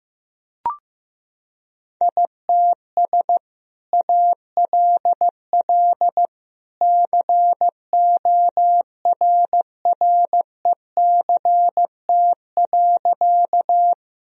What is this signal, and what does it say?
Morse code, heard as a keyed tone, about 15 wpm, 700 Hz. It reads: ITS ALL CORRECT.
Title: Sentences from Top 1000 Words - Review Part 2 - 15wpm